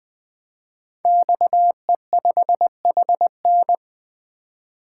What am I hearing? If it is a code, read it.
Morse code, heard as a keyed tone, about 20 wpm, 700 Hz. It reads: XE5HN
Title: Code Groups - 5 Letter:Numbers - 20wpm